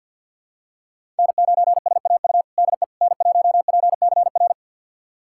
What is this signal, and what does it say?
Morse code, heard as a keyed tone, about 51 wpm, 700 Hz. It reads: N9SAU DE N1PCR
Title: USA Callsign DE Callsign - 5 Characters - 50wpm